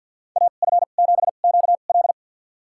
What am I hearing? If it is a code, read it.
Morse code, heard as a keyed tone, about 46 wpm, spaced as for 35 wpm, 700 Hz. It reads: AF7QL
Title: USA Call Signs - 2x Character Spacing - 50wpm